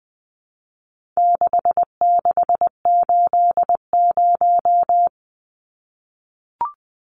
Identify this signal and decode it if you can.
Morse code, heard as a keyed tone, about 20 wpm, 700 Hz. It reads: 6680
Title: Numbers - 4-Digit 20wpm